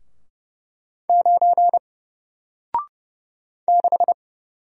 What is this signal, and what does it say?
Morse code, heard as a keyed tone, about 30 wpm, 700 Hz. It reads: 9 6
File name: Single-Letter-Number-30wpm